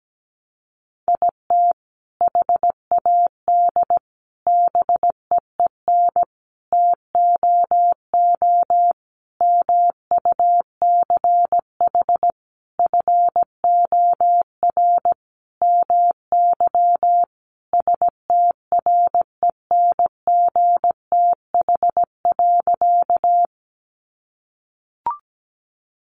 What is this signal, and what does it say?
Morse code, heard as a keyed tone, about 17 wpm, 700 Hz. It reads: IT HAD BEEN TOO MUCH FOR MY STRENGTH.
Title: Sentences from Top 1300 Words 17wpm